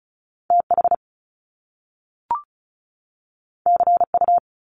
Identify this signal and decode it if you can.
Morse code, heard as a keyed tone, about 36 wpm, 700 Hz. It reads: TH CU
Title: Top Two Letter Combinations in Words 35wpm